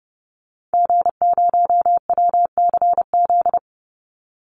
Morse code, heard as a keyed tone, about 30 wpm, 700 Hz. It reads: G0WCZ
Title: Call Signs - 30wpm